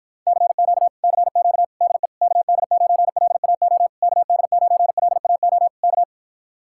Morse code, heard as a keyed tone, about 52 wpm, 700 Hz. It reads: CQ CQ DE KD9LAO KD9LAO K